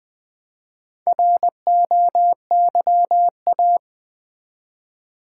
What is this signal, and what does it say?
Morse code, heard as a keyed tone, about 21 wpm, 700 Hz. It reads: ROYA
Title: Build A Word - 1 Letter At A Time - 3 to 5-Letter Words - 20wpm